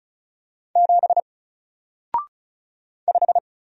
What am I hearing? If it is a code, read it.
Morse code, heard as a keyed tone, about 36 wpm, 700 Hz. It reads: 7 5
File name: Single-Letter-Number-35wpm